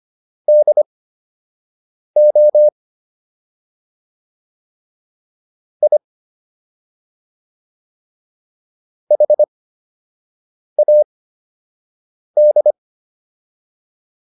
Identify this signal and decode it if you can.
Morse code, heard as a keyed tone, about 25 wpm, spaced as for 6 wpm, 600 Hz. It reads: DO I HAD